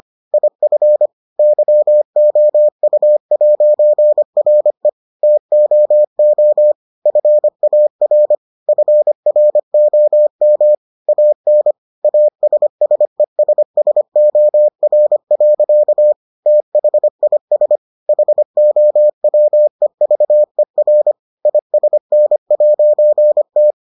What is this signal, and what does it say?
Morse code, heard as a keyed tone, about 25 wpm, 600 Hz. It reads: IF YOU'RE TOO FAR FROM AN ASSESSOR. THIS HOWEVER ISN'T